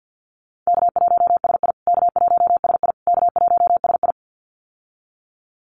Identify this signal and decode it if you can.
Morse code, heard as a keyed tone, about 48 wpm, 700 Hz. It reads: K1SI K1SI K1SI
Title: USA Call Signs Repeated 3x 50wpm